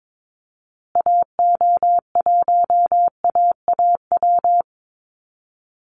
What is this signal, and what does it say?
Morse code, heard as a keyed tone, about 22 wpm, 700 Hz. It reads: AO1AAW
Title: Call Signs Repeated 3x - 22wpm